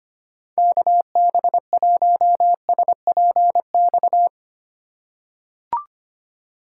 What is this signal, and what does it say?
Morse code, heard as a keyed tone, about 25 wpm, 700 Hz. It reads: KB1SPX
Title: USA Call Signs - 2x Character Spacing - 25wpm